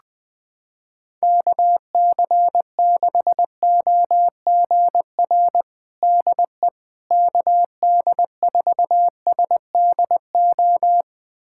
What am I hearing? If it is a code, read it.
Morse code, heard as a keyed tone, about 20 wpm, 700 Hz. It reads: KC6OGR DE KD4SDO
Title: USA Callsign DE Callsign 20wpm